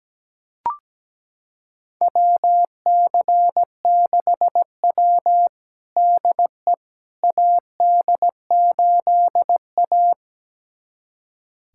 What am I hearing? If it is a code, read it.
Morse code, heard as a keyed tone, about 17 wpm, 700 Hz. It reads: WC6W DE AD8A
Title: USA Callsign DE Callsign - 4 Characters - 17wpm